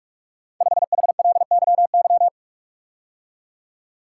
Morse code, heard as a keyed tone, about 44 wpm, 700 Hz. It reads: 5HLYY
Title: Code Groups - 5 Letter:Numbers - 45wpm